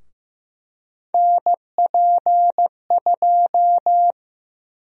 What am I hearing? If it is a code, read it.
Morse code, heard as a keyed tone, about 15 wpm, 700 Hz. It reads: NP2
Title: USA Call Sign Prefixes - Repeated 2x - 15wpm